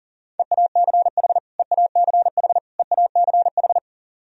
Morse code, heard as a keyed tone, about 40 wpm, 700 Hz. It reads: EACH EACH EACH
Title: Top 100 Words Repeated 40wpm